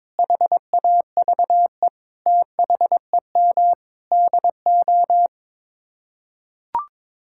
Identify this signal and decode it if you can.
Morse code, heard as a keyed tone, about 22 wpm, 700 Hz. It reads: HAVE THEM DO
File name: Sets of 9 Words - 22wpm